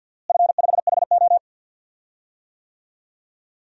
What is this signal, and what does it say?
Morse code, heard as a keyed tone, about 47 wpm, 700 Hz. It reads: F5HO
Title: DXCC-100 Call Signs - 4 Characters - Repeated 3x - 50wpm